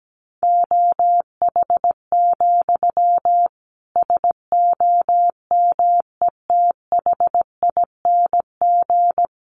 Morse code, heard as a keyed tone, about 17 wpm, 700 Hz. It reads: OH, SOMETHING
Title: Sentences from Top 1500 Words 17wpm